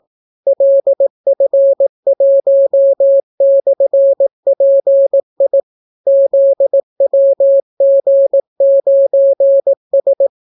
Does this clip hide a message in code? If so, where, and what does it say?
Yes: Morse code from 0.5 to 10.3 s, heard as a keyed tone, about 18 wpm, 550 Hz. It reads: LF1/PI ZWG9S